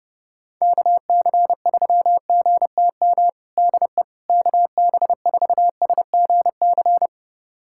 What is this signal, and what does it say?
Morse code, heard as a keyed tone, about 30 wpm, 700 Hz. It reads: KC3GTM DE KB4SGC
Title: USA Callsign DE Callsign - 6 Characters - 30wpm